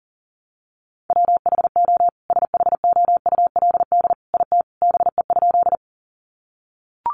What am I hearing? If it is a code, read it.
Morse code, heard as a keyed tone, about 40 wpm, 700 Hz. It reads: WHO SHOULD IT BE?
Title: Sentences from Top 300 Words 40wpm